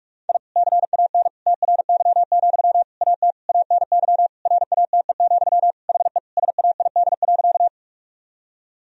Morse code, heard as a keyed tone, about 47 wpm, 700 Hz. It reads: I CAN TRY, AT ANY RATE, HE SAID.
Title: Sentences from Top 1000 Words - Review Part 4 - 45wpm